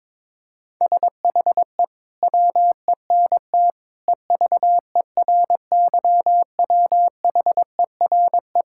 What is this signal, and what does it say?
Morse code, heard as a keyed tone, about 21 wpm, 700 Hz. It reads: SHE WENT EVERYWHERE
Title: Sentences from Top 1600 Words 22wpm